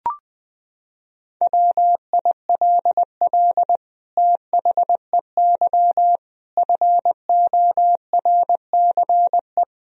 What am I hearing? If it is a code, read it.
Morse code, heard as a keyed tone, about 20 wpm, 700 Hz. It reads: WILL THEY FORCE
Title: Sentences from Top 1000 Words - Review Part 3 - 20wpm